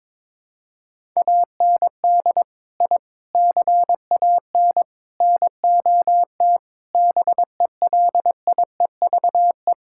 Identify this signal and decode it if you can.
Morse code, heard as a keyed tone, about 22 wpm, 700 Hz. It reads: AND I CAN NOT BELIEVE
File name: practice / 22wpm / Sentences with 4-Word Phrases - 22wpm